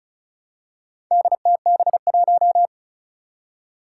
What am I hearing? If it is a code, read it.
Morse code, heard as a keyed tone, about 36 wpm, 700 Hz. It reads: DTB1
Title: Four Challenging Letter-Number Combinations 35wpm